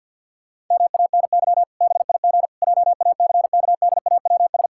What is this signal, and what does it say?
Morse code, heard as a keyed tone, about 48 wpm, 700 Hz. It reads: MANY BIG JACKDAWS
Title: Pangram Challenge v2 - 50wpm